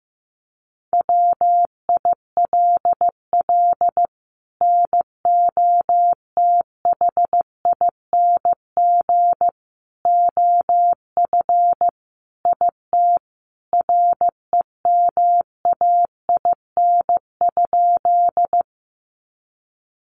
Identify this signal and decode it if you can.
Morse code, heard as a keyed tone, about 15 wpm, 700 Hz. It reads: WILL NOTHING OF IT REMAIN?
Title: Sentences from Top 1200 Words 15wpm